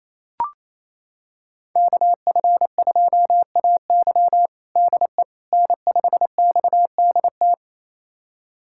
Morse code, heard as a keyed tone, about 28 wpm, 700 Hz. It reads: KF2AY DE N5XDT
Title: USA Callsign DE Callsign - 5 Characters - 28wpm